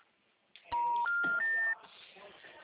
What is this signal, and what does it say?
Special information tone.
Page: Rio Claro 1